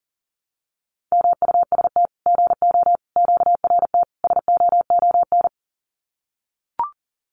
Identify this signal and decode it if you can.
Morse code, heard as a keyed tone, about 40 wpm, 700 Hz. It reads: MUST GO QRT SOON
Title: Common QSO Phrases 40wpm